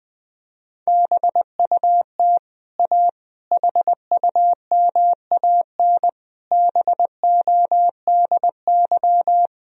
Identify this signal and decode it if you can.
Morse code, heard as a keyed tone, about 20 wpm, 700 Hz. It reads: BUT A HUMAN BODY
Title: Sentences from Top 1000 Words - Review Part 3 - 20wpm